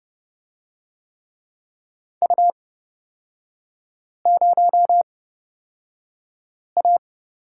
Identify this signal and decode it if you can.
Morse code, heard as a keyed tone, about 30 wpm, spaced as for 5 wpm, 700 Hz. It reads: U0A
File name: Lesson 36e - BT - Three Characters 30wpm